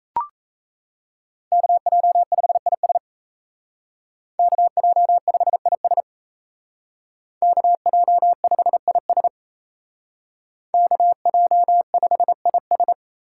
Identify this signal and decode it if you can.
Morse code, heard as a keyed tone, about 33 wpm, 700 Hz. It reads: KJ5IS KJ5IS KJ5IS KJ5IS
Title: Speed Racing - USA 2x2 Call Signs 28wpm